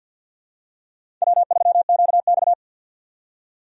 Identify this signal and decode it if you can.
Morse code, heard as a keyed tone, about 48 wpm, 700 Hz. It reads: W3QX
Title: USA Call Signs - 4 Characters - 50wpm